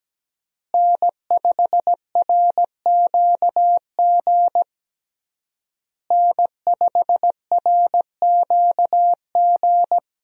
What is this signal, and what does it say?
Morse code, heard as a keyed tone, about 17 wpm, 700 Hz. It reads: N5RQG N5RQG
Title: USA Call Signs - 5 Characters - Repeated 3x - 17wpm